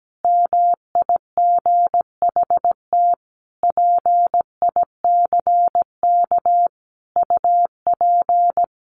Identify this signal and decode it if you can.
Morse code, heard as a keyed tone, about 17 wpm, 700 Hz. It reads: MIGHT PICK UP
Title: A Sound Of Thunder 17wpm